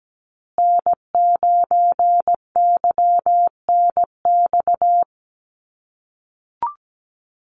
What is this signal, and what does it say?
Morse code, heard as a keyed tone, about 17 wpm, 700 Hz. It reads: N9YNX
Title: USA Call Signs Repeated 2x 17wpm